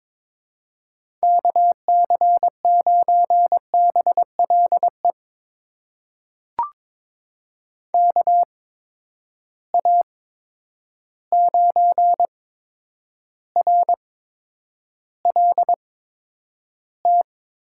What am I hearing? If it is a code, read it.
Morse code, heard as a keyed tone, about 22 wpm, spaced as for 9 wpm, 700 Hz. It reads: KC9BLE KA9RLT
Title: USA Call Signs - 8x Character Spacing - 22wpm